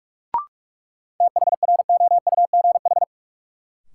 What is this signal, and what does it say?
Morse code, heard as a keyed tone, about 46 wpm, 700 Hz. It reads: THROUGH
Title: Sentences from Top 200 Words - New Word Drill - 45wpm